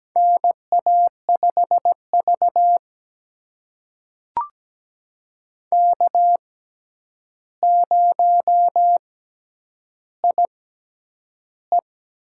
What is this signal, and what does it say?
Morse code, heard as a keyed tone, about 17 wpm, spaced as for 9 wpm, 700 Hz. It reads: NA5V K0IE
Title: USA Call Signs - 6x Character Spacing - 17wpm